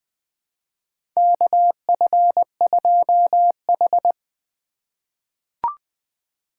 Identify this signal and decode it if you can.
Morse code, heard as a keyed tone, about 21 wpm, 700 Hz. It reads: KF2H
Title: USA Call Signs - 4 Characters - 4x Character Spacing - 20wpm